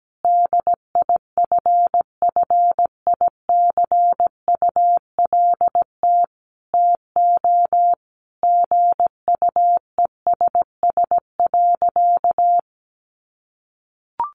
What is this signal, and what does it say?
Morse code, heard as a keyed tone, about 17 wpm, 700 Hz. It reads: DIFFICULT TO GUESS.
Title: Sentences from Top 1500 Words 17wpm